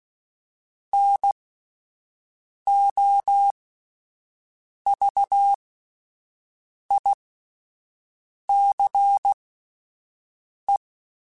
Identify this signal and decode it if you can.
Morse code, heard as a keyed tone, about 16 wpm, spaced as for 5 wpm, 800 Hz. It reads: NOVICE